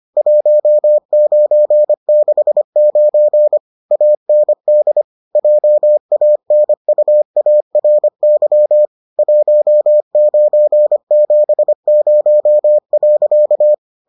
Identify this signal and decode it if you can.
Morse code, heard as a keyed tone, about 25 wpm, 600 Hz. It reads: 1969 AND JANUARY 1970.